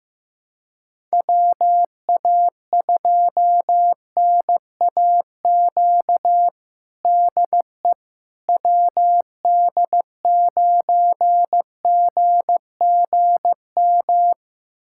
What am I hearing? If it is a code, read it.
Morse code, heard as a keyed tone, about 16 wpm, 700 Hz. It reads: WA2NAQ DE WD9GGM
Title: USA Callsign DE Callsign - 6 Characters - 15wpm